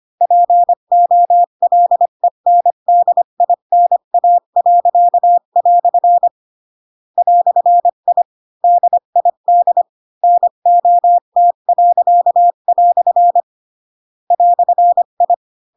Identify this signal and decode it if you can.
Morse code, heard as a keyed tone, about 25 wpm, 700 Hz. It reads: POLENDINA." "I DID NOT." "I